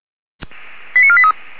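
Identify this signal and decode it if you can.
Five-tone selective call (ZVEI), heard as quick runs of five tones, tones of about 70 ms. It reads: 89483
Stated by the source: ZVEI five-tone selective call received with an SDR.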